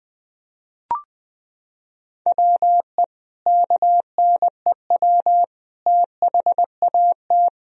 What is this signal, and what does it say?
Morse code, heard as a keyed tone, about 20 wpm, 700 Hz. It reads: WE KNEW THAT
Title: Sentences from Top 400 Words and Abbreviations 20wpm